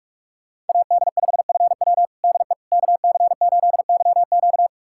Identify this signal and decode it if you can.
Morse code, heard as a keyed tone, about 46 wpm, 700 Hz. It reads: AD5FW DE KC8YQ
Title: USA Callsign DE Callsign - 5 Characters - 45wpm